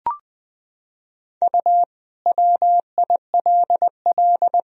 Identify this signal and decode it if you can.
Morse code, heard as a keyed tone, about 21 wpm, 700 Hz. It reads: U WILL
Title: Sentences from Top 500 Words and Abbreviations 20wpm